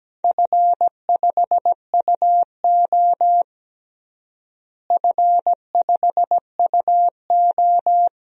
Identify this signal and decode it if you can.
Morse code, heard as a keyed tone, about 17 wpm, 700 Hz. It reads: F5UO F5UO
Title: DXCC-100 Call Signs - 4 Characters - Repeated 2x - 17wpm